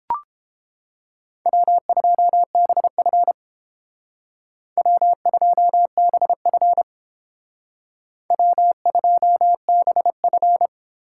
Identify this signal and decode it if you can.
Morse code, heard as a keyed tone, about 29 wpm, 700 Hz. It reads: W2BF W2BF W2BF